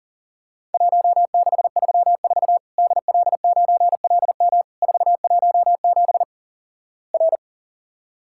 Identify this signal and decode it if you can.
Morse code, heard as a keyed tone, about 38 wpm, 700 Hz. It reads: 1634 DL9LM 417 R
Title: Worked All Europe DX Contest - QTC Exchange - 40wpm